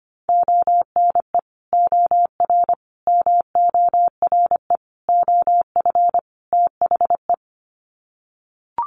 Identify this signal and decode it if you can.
Morse code, heard as a keyed tone, about 25 wpm, 700 Hz. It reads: ONE OR MORE OF THE
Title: Sets of 5 Words Encore 25wpm